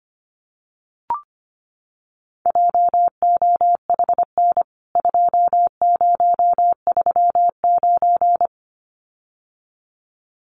Morse code, heard as a keyed tone, about 25 wpm, 700 Hz. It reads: JOHN 2039
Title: CWT Exchange 25wpm